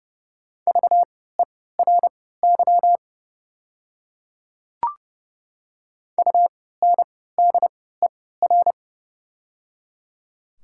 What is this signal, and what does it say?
Morse code, heard as a keyed tone, about 30 wpm, spaced as for 17 wpm, 700 Hz. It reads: VERY UNDER
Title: 3x Character Spacing - Top 200 Words - 30wpm